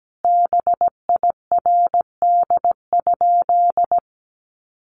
Morse code, heard as a keyed tone, about 17 wpm, 700 Hz. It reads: BIRD?
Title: Sentences from Top 400 Words 17wpm